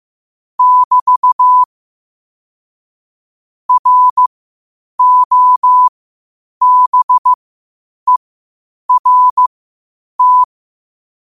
Morse code, heard as a keyed tone, about 15 wpm, spaced as for 8 wpm, 1000 Hz. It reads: <BT> ROBERT
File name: qotdfiles / 8wpm